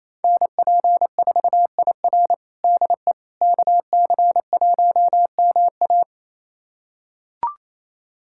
Morse code, heard as a keyed tone, about 28 wpm, 700 Hz. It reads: NP4IR DE KC1MA
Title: USA Callsign DE Callsign - 5 Characters - 28wpm